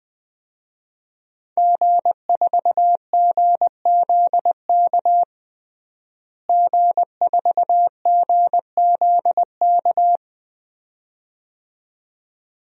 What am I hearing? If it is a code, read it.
Morse code, heard as a keyed tone, about 21 wpm, 700 Hz. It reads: G4GZK G4GZK
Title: DXCC-100 Call Signs - 5 Characters - Repeated 2x - 20wpm